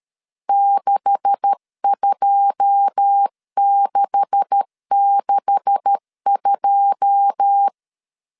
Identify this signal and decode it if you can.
Morse code, heard as a keyed tone, about 13 wpm, 800 Hz. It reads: 62662
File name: LEZIONI / 9 286 12wpm 2305